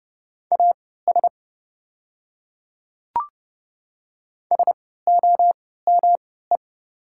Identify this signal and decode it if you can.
Morse code, heard as a keyed tone, about 30 wpm, spaced as for 17 wpm, 700 Hz. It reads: AS SOME
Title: 3x Character Spacing - Top 100 Words - 30wpm